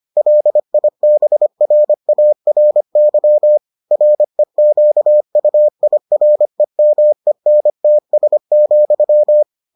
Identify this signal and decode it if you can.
Morse code, heard as a keyed tone, about 25 wpm, 600 Hz. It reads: LIBRARY REQUIREMENTS,